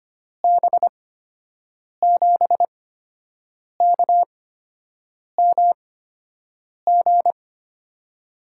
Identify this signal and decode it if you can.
Morse code, heard as a keyed tone, about 25 wpm, spaced as for 7 wpm, 700 Hz. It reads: B7KMG
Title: USA Call Signs - 6 Characters - 8x Character Spacing - 25wpm